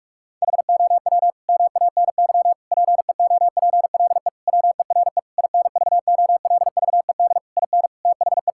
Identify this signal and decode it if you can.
Morse code, heard as a keyed tone, about 46 wpm, 700 Hz. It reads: HOW MANY PEOPLE WERE INVOLVED IN THE
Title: Sentences from Top 1000 Words - Review Part 5 - 45wpm